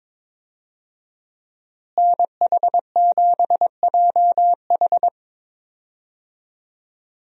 Morse code, heard as a keyed tone, about 21 wpm, 700 Hz. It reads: NH7JH
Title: USA Call Signs - 5 Characters - Repeated 2x - 22wpm